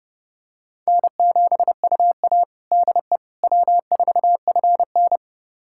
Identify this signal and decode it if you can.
Morse code, heard as a keyed tone, about 32 wpm, 700 Hz. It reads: N7UA DE W4FN
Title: USA Callsign DE Callsign - 4 Characters - 30wpm